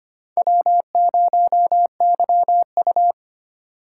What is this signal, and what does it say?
Morse code, heard as a keyed tone, about 25 wpm, 700 Hz. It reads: W0YU
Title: USA Call Signs - 4x Character Spacing - 25wpm